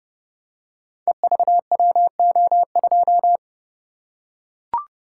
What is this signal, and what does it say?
Morse code, heard as a keyed tone, about 30 wpm, 700 Hz. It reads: EVWO2
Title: Code Groups - 5 Letter:Numbers - 30wpm